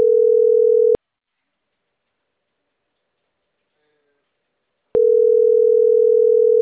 Ringback tone (North America), 440 + 480 Hz.